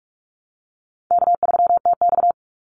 Morse code, heard as a keyed tone, about 44 wpm, 700 Hz. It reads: K3TX
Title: Easy Call Signs Repeated 3x - 45wpm